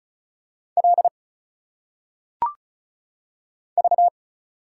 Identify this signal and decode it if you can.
Morse code, heard as a keyed tone, about 34 wpm, 700 Hz. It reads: L V
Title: Single Letters - Phonetic Pronunciation 35wpm